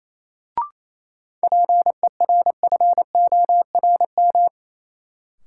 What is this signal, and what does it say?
Morse code, heard as a keyed tone, about 27 wpm, 700 Hz. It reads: PERFORM
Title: Sentences from Top 1500 Words - New Word Drill Alt - 28wpm